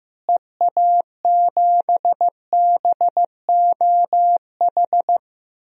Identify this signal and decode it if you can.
Morse code, heard as a keyed tone, about 16 wpm, 700 Hz. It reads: EA7BOH
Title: DXCC-100 Call Signs - 6 Characters - Repeated 3x - 15wpm